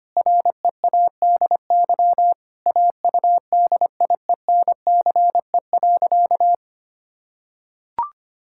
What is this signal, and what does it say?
Morse code, heard as a keyed tone, about 25 wpm, 700 Hz. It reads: READY AUDIENCE.
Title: Sentences from Top 1000 Words - Review Part 5 - 25wpm